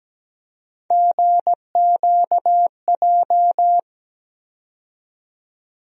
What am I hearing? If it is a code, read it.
Morse code, heard as a keyed tone, about 17 wpm, 700 Hz. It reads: GQJ
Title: USA Call Sign Postfixes - 3 Letters 17wpm